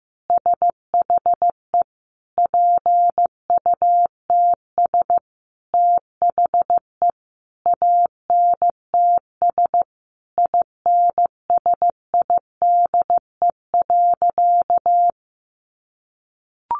Morse code, heard as a keyed tone, about 15 wpm, 700 Hz. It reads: SHE PUTS THE ANTS INSIDE.
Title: Additional Easy Phrases 15wpm